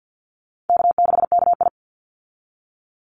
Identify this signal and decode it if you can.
Morse code, heard as a keyed tone, about 50 wpm, 700 Hz. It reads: K6KI
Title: Easy Call Signs Repeated 3x - 50wpm